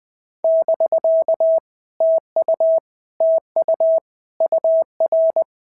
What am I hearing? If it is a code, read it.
Morse code, heard as a keyed tone, about 20 wpm, 650 Hz. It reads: <BK> TU TU UR
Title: POTA QSO Practice - 20wpm